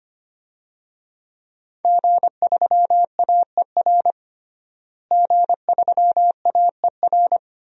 Morse code, heard as a keyed tone, about 25 wpm, 700 Hz. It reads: G3AER G3AER
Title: DXCC-100 Call Signs - 5 Characters - Repeated 3x - 25wpm